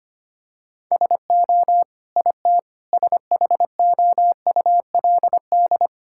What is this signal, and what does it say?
Morse code, heard as a keyed tone, about 25 wpm, 700 Hz. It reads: SO IT SHOULD
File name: Sets of 9 Words - 25wpm